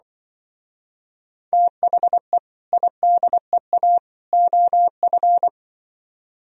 Morse code, heard as a keyed tone, about 25 wpm, 700 Hz. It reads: THE IDEA OF
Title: Speed-Racing - Sets of 3 Words 20wpm